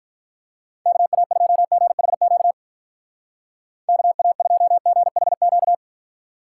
Morse code, heard as a keyed tone, about 44 wpm, 700 Hz. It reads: KA2GHQ KA2GHQ
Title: Speed Racing - USA 2x3 Call Signs 35wpm